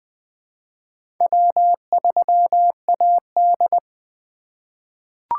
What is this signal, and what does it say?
Morse code, heard as a keyed tone, about 20 wpm, 700 Hz. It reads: W3AD
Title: USA Call Signs - 2x Character Spacing - 20wpm